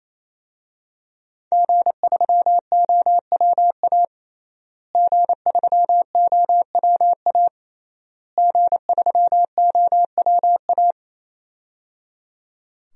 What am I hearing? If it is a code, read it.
Morse code, heard as a keyed tone, about 28 wpm, 700 Hz. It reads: G3OWA G3OWA G3OWA
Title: DXCC-100 Call Signs - 5 Characters - Repeated 3x - 28wpm